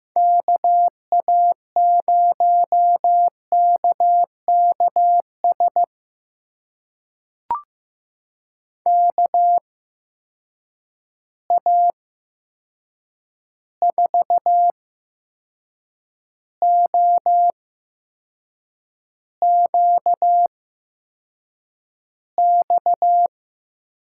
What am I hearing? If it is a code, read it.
Morse code, heard as a keyed tone, about 15 wpm, spaced as for 6 wpm, 700 Hz. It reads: KA0KKS KA4OQX
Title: USA Call Signs - 8x Character Spacing - 15wpm